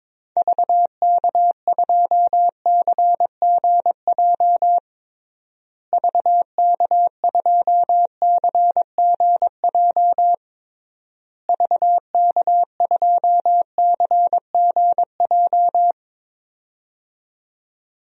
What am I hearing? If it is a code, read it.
Morse code, heard as a keyed tone, about 21 wpm, 700 Hz. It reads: VK2CGJ VK2CGJ VK2CGJ
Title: DXCC-100 Call Signs - 6 Characters - Repeated 3x - 22wpm